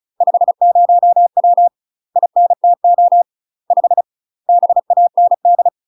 Morse code, heard as a keyed tone, about 36 wpm, 700 Hz. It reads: 50W INTO 5 BAND
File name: UK MASTER QSOs TRAINING / 35WPM / UK MASTER QSOs 0032